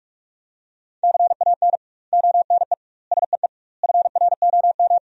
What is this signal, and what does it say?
Morse code, heard as a keyed tone, about 41 wpm, 700 Hz. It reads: CAN ONE SEE FROM
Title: 2x Word-Spacing - Sentences from Top 100 Words - 45wpm